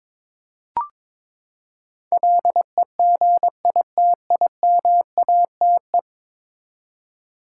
Morse code, heard as a keyed tone, about 22 wpm, 700 Hz. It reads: LEGITIMATE
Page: morse-code-ninja-practice-22wpm